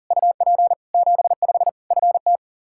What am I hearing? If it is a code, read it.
Morse code, heard as a keyed tone, about 37 wpm, 700 Hz. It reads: UP 75 FT
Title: K1USN SST QSO Practice 40wpm